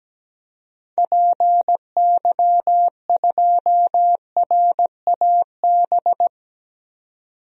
Morse code, heard as a keyed tone, about 17 wpm, 700 Hz. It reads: PY2RAB
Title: DXCC-100 Call Signs - 6 Characters - 17wpm